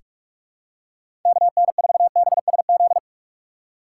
Morse code, heard as a keyed tone, about 47 wpm, 700 Hz. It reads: KN4BSZ
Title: USA Call Signs - 6x Character Spacing - 45wpm